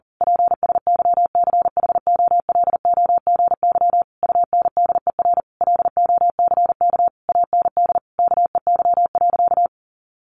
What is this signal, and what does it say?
Morse code, heard as a keyed tone, about 40 wpm, 700 Hz. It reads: PSYCHOLOGY UNDER LOCK AND KEY.
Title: Sentences from Top 1400 Words 40wpm